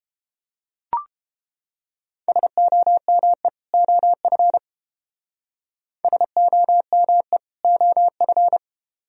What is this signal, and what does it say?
Morse code, heard as a keyed tone, about 31 wpm, 700 Hz. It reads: SOME OF SOME OF